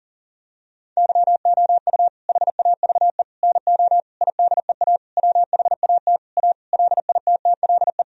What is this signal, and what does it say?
Morse code, heard as a keyed tone, about 40 wpm, 700 Hz. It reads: YOU HAVE NO IDEA WHAT A LITTLE
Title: Sentences from Top 1000 Words - Review Part 1 - 40wpm